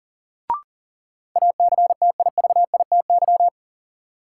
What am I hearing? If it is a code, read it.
Morse code, heard as a keyed tone, about 40 wpm, 700 Hz. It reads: ACTIVITY
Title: Sentences from Top 1000 Words - Review Part 4 - New Word Drill - 40wpm